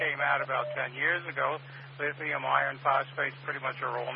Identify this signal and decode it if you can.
Emergency Alert System (SAME) tones, heard as harsh buzzing bursts, with speech. It is heard from 0.9 to 3.6 s.